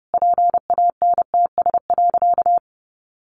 Morse code, heard as a keyed tone, about 30 wpm, 700 Hz. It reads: PANTS.
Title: Additional Easy Phrases 30wpm